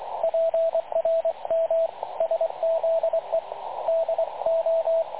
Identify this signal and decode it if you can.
Morse code, heard as a keyed tone, about 22 wpm, 660 Hz, with radio noise. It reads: PRM SZE DO
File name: dlazaawansowanych / nagraniazpasma / websdr recording 3567.0kHz